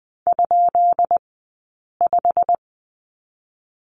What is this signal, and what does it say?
Morse code, heard as a keyed tone, about 20 wpm, spaced as for 8 wpm, 700 Hz. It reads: ?5
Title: 144 Lesson 24d - Letter y - Two Characters 20wpm